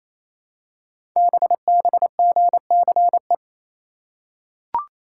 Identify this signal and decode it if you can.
Morse code, heard as a keyed tone, about 27 wpm, 700 Hz. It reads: BBGCE
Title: Code Groups - 5 Letter:Numbers - 28wpm